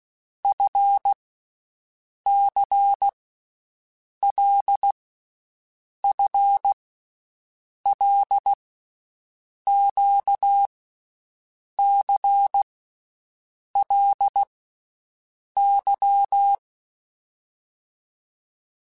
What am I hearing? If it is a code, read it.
Morse code, heard as a keyed tone, about 16 wpm, 800 Hz. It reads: F C L F L Q C L Y
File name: Groups@10 / Groups@10 mp3s / LFCYQ@10